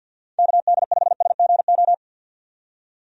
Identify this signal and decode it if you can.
Morse code, heard as a keyed tone, about 46 wpm, 700 Hz. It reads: KD5SGO
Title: USA Call Signs - 6 Characters - 50wpm